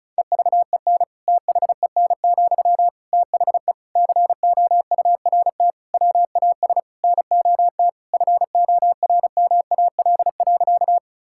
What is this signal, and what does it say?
Morse code, heard as a keyed tone, about 36 wpm, 700 Hz. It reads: EVEN THEN, THE COURT WAS NOT FORMAL.
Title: Sentences from Top 1600 Words 35wpm